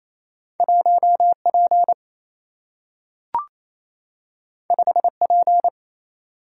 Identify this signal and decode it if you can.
Morse code, heard as a keyed tone, about 28 wpm, 700 Hz. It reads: 1 P 5 P
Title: Single-Letter Single-Number 28wpm